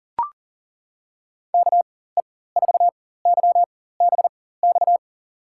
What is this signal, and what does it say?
Morse code, heard as a keyed tone, about 40 wpm, spaced as for 19 wpm, 700 Hz. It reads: KE4YBX
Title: USA Call Signs - 6 Characters - 4x Character Spacing - 40wpm